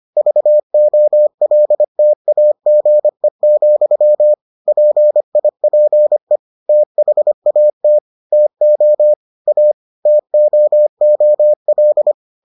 Morse code, heard as a keyed tone, about 25 wpm, 600 Hz. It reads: VOLTAGE, PIPE THAT TO A TOOL